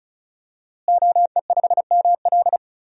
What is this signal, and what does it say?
Morse code, heard as a keyed tone, about 34 wpm, 700 Hz. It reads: OE5ML
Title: DXCC-100 Call Signs - 5 Characters - Repeated 3x - 35wpm